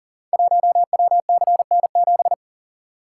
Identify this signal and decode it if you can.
Morse code, heard as a keyed tone, about 39 wpm, 690 Hz. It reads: 1WCN7
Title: Code Groups - 5 Letter:Numbers - 40wpm